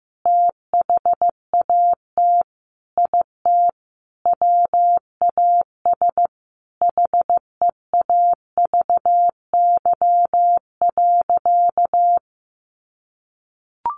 Morse code, heard as a keyed tone, about 15 wpm, 700 Hz. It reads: THAT IT WAS HEAVY.
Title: Sentences from Top 1100 Words 15wpm